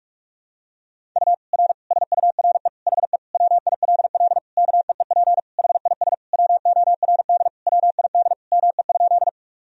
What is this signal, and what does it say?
Morse code, heard as a keyed tone, about 46 wpm, 700 Hz. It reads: U R SURE HE WILL KEEP HIS WORD WID ME?
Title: Sentences from Top 300 Words and Abbreviations 45wpm